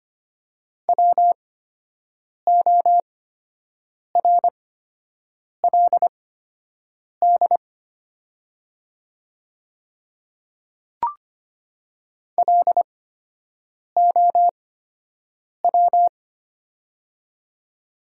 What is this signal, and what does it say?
Morse code, heard as a keyed tone, about 25 wpm, spaced as for 7 wpm, 700 Hz. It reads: WORLD LOW
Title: 8x Character Spacing - Top 300 Words - 25wpm